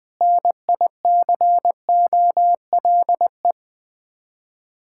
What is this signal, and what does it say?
Morse code, heard as a keyed tone, about 20 wpm, 700 Hz. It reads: NICOLE
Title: Practice QSO - Exchange 1 - Beginning - 20wpm